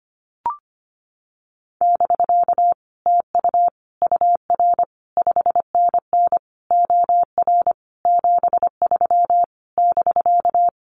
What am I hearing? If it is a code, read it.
Morse code, heard as a keyed tone, about 25 wpm, 700 Hz. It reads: <BK> TU UR 5NN OR 73 <BK>
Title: Common QSO Phrases Encore 25wpm